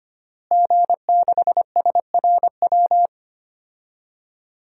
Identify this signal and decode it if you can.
Morse code, heard as a keyed tone, about 25 wpm, 700 Hz. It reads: G6SRW
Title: DXCC-100 Call Signs - 5 Characters - Repeated 2x - 25wpm